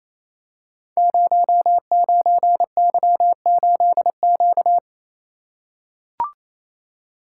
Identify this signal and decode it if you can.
Morse code, heard as a keyed tone, about 29 wpm, 700 Hz. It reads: 09Y8Q
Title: Code Groups - 5 Letter:Numbers - 28wpm